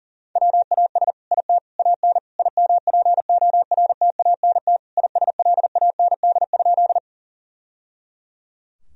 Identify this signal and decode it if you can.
Morse code, heard as a keyed tone, about 40 wpm, 700 Hz. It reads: WAS IT AN IMPORTANT ISLAND?
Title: Sentences from Top 1000 Words - Review Part 3 - 40wpm